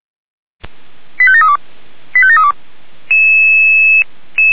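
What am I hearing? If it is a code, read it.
Five-tone selective call (ZVEI), heard as quick runs of five tones, tones of about 80 ms. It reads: 85732 85732
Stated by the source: ZVEI five-tone selective call received via SDR.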